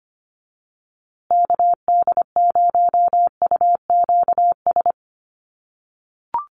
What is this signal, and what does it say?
Morse code, heard as a keyed tone, about 25 wpm, 700 Hz. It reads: KD0UQS
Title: USA Call Signs Repeated 3x 25wpm